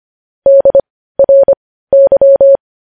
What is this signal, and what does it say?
Morse code, heard as a keyed tone, about 25 wpm, spaced as for 15 wpm, 550 Hz. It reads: DRY
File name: podcast / News headlines-1x-25x15